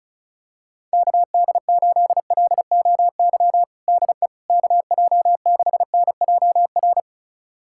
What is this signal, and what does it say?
Morse code, heard as a keyed tone, about 36 wpm, 700 Hz. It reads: KD8LOY DE KJ6NJR
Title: USA Callsign DE Callsign - 6 Characters - 35wpm